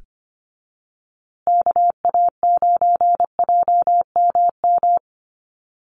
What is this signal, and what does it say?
Morse code, heard as a keyed tone, about 25 wpm, 700 Hz. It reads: KA9JMM
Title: USA Call Signs Repeated 2x 25wpm